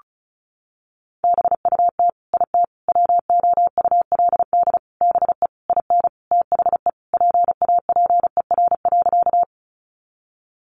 Morse code, heard as a keyed tone, about 34 wpm, 700 Hz. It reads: BUT IT WOULD BE IN THE PAPER.
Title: Sentences from Top 300 Words 35wpm